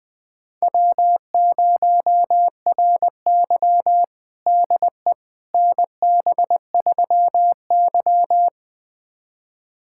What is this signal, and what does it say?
Morse code, heard as a keyed tone, about 20 wpm, 700 Hz. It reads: W0RY DE NB3Y
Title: USA Callsign DE Callsign - 4 Characters - 20wpm